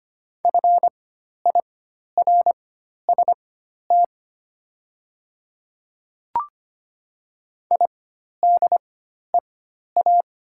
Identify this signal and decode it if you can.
Morse code, heard as a keyed tone, about 25 wpm, spaced as for 12 wpm, 700 Hz. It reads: FIRST IDEA
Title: 4x Character Spacing - Top 500 Words - 25wpm